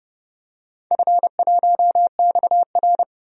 Morse code, heard as a keyed tone, about 30 wpm, 700 Hz. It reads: F1XR
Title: DXCC-100 Call Signs - 4 Characters - 30wpm